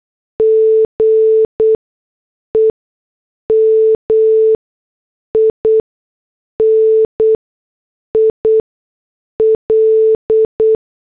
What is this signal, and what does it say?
Morse code, heard as a keyed tone, about 8 wpm, spaced as for 6 wpm, 440 Hz. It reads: GEMINI L